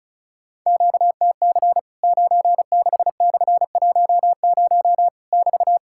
Morse code, heard as a keyed tone, about 36 wpm, 700 Hz. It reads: QTC 96/10 <BT>
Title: Worked All Europe DX Contest - QTC Exchange - 35wpm